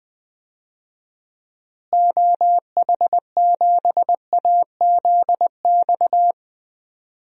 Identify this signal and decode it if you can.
Morse code, heard as a keyed tone, about 20 wpm, 700 Hz. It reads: OH7AZX
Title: DXCC-100 Call Signs - 6 Characters - Repeated 2x - 20wpm